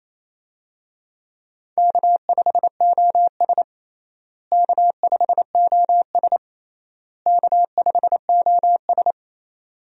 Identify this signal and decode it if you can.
Morse code, heard as a keyed tone, about 28 wpm, 700 Hz. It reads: K5OS K5OS K5OS
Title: USA Call Signs - 4 Characters - Repeated 3x - 28wpm